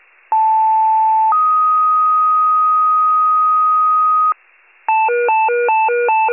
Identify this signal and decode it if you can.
Two-tone page, heard as a one-second tone, then a three-second tone, about 880 then 1250 Hz. It is heard from 0.3 to 4.3 s.
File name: pager / 76 FIRE / wcalert